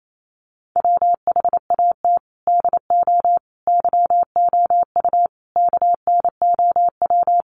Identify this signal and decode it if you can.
Morse code, heard as a keyed tone, about 28 wpm, 700 Hz. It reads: WHAT DO YOU KNOW
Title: Sentences from Top 600 Words 28wpm